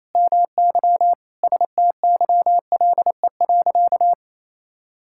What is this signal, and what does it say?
Morse code, heard as a keyed tone, about 28 wpm, 700 Hz. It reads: MY STYLE.
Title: Sentences from Top 1000 Words - Review Part 5 - 28wpm